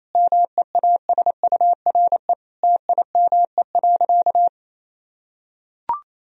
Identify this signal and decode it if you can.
Morse code, heard as a keyed tone, about 28 wpm, 700 Hz. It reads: MEASURE TIME.
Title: Sentences from Top 1000 Words - Review Part 2 - 28wpm